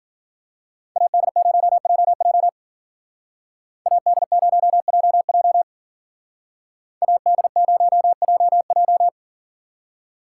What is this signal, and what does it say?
Morse code, heard as a keyed tone, about 48 wpm, 700 Hz. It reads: AD0JJ AD0JJ AD0JJ
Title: Speed Racing - USA Call Signs 40wpm